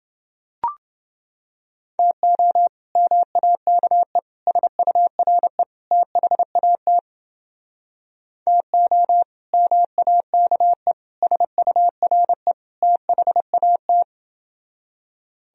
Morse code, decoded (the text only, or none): TO MAKE SURE THAT TO MAKE SURE THAT